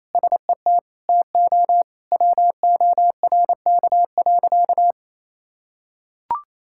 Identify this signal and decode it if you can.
Morse code, heard as a keyed tone, about 27 wpm, 700 Hz. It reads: SET TO WORK.
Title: Sentences from Top 1000 Words - Review Part 6 - 28wpm